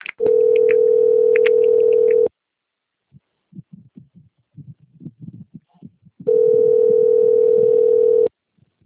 Ringback tone (North America), 440 + 480 Hz.